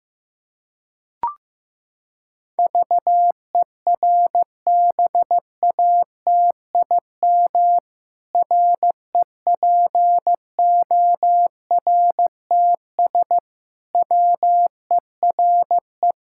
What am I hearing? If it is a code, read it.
Morse code, heard as a keyed tone, about 15 wpm, 700 Hz. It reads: VERBATIM REPORTS WERE
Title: Pangram Challenge v2 - 15wpm